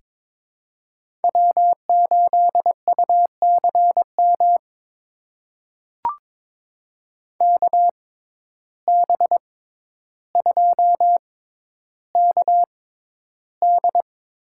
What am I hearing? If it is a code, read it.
Morse code, heard as a keyed tone, about 22 wpm, spaced as for 11 wpm, 700 Hz. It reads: W8UCM KB2KD
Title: USA Call Signs - 6x Character Spacing - 22wpm